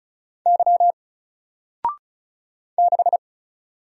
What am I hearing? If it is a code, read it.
Morse code, heard as a keyed tone, about 36 wpm, 700 Hz. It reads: Y 6
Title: Single Letter-Number Alt 35wpm